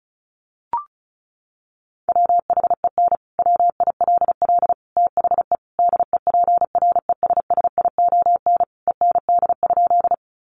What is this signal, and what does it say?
Morse code, heard as a keyed tone, about 34 wpm, 700 Hz. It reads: WHEN WILL THE DEPRESSION END?
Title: Sentences from Top 1300 Words 35wpm